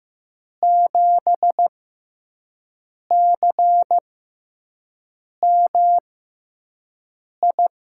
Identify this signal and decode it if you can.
Morse code, heard as a keyed tone, about 15 wpm, spaced as for 5 wpm, 700 Hz. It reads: 7CMI
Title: USA Call Signs - 6 Characters - 6x Character Spacing - 15wpm